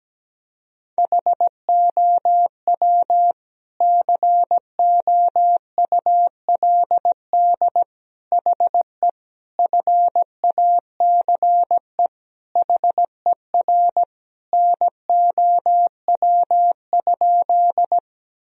Morse code, heard as a keyed tone, about 17 wpm, 700 Hz. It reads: HOW COULD HE FACE HER NOW?
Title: Sentences from Top 1000 Words - Review Part 2 - 17wpm